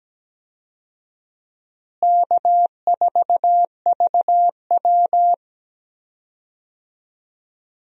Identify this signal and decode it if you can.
Morse code, heard as a keyed tone, about 17 wpm, 700 Hz. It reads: K4VW
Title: USA Call Signs - 4 Characters - Repeated 2x - 17wpm